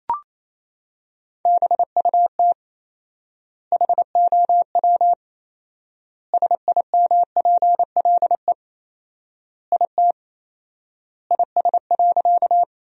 Morse code, heard as a keyed tone, about 27 wpm, 700 Hz. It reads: BUT HOW SIMPLE IT IS.
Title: 4x Word-Spacing - Sentences from Top 400 Words - 28wpm